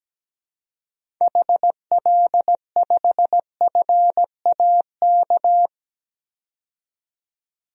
Morse code, heard as a keyed tone, about 17 wpm, 700 Hz. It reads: HL5FAK
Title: DXCC-100 Call Signs - 6 Characters - 17wpm